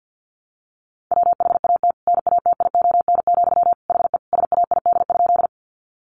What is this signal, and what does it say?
Morse code, heard as a keyed tone, about 50 wpm, 700 Hz. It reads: WHAT NATION, HE SAID?
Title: Sentences from Top 900 Words 50wpm